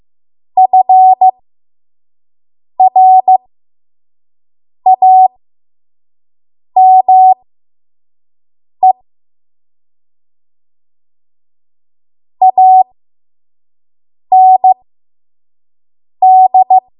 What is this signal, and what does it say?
Morse code, heard as a keyed tone, about 15 wpm, spaced as for 5 wpm, 750 Hz. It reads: FRAME AND
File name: Morse / 05WPM